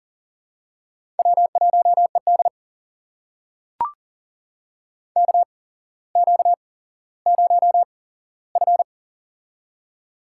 Morse code, heard as a keyed tone, about 38 wpm, spaced as for 17 wpm, 700 Hz. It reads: W1ED KQ0F
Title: USA Call Signs - 4 Characters - 8x Character Spacing - 40wpm